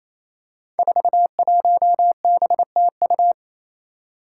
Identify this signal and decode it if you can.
Morse code, heard as a keyed tone, about 27 wpm, 700 Hz. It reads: 41BTU
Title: Five Challenging Letter-Number Combinations 28wpm